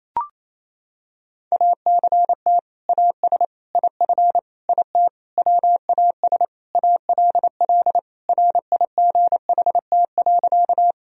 Morse code, heard as a keyed tone, about 28 wpm, 700 Hz. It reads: ACT AS IF IT WAS ALL RIGHT.
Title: Sentences from Top 1000 Words - Review Part 1 - 28wpm